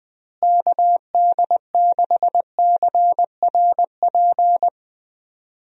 Morse code, heard as a keyed tone, about 21 wpm, 700 Hz. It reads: KD6CRP
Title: USA Call Signs - 6 Characters - Repeated 3x - 20wpm